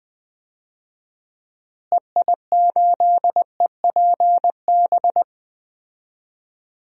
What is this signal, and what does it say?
Morse code, heard as a keyed tone, about 20 wpm, 700 Hz. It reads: EI8EPB
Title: DXCC-100 Call Signs - 6 Characters - Repeated 2x - 20wpm